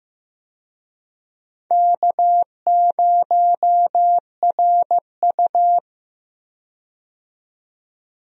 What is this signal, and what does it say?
Morse code, heard as a keyed tone, about 15 wpm, 700 Hz. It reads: K0RU
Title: USA Call Signs - 4 Characters - 15wpm